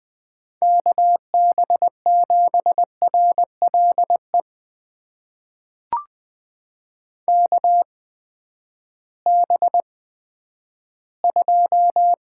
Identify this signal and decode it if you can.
Morse code, heard as a keyed tone, about 20 wpm, spaced as for 11 wpm, 700 Hz. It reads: KB7RLE KB2
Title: USA Call Signs - 6 Characters - 8x Character Spacing - 20wpm